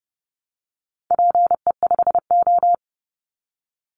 Morse code, heard as a keyed tone, about 30 wpm, 700 Hz. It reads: PE5O
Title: Call Signs - 30wpm